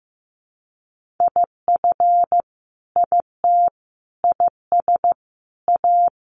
Morse code, heard as a keyed tone, about 15 wpm, 700 Hz. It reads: IF IT IS A
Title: Sentences from Top 1500 Words 15wpm